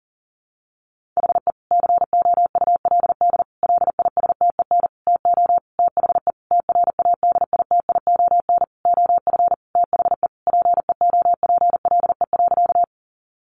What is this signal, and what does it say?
Morse code, heard as a keyed tone, about 40 wpm, 700 Hz. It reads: HE COULD LISTEN TO THE TRADITION OF THE PEOPLE.
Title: Sentences from Top 1500 Words 40wpm